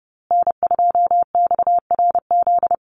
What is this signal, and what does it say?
Morse code, heard as a keyed tone, about 30 wpm, 700 Hz. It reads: N2XRZ
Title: USA Call Signs Repeated 3x 30wpm